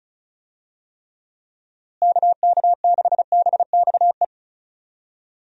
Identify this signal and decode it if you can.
Morse code, heard as a keyed tone, about 34 wpm, 700 Hz. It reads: KK6BXE
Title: USA Call Signs - 6 Characters - Repeated 3x - 35wpm